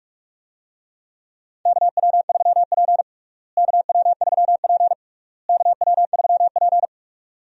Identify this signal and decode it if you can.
Morse code, heard as a keyed tone, about 46 wpm, 700 Hz. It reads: KW3P KW3P KW3P
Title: USA Call Signs - 4 Characters - Repeated 3x - 45wpm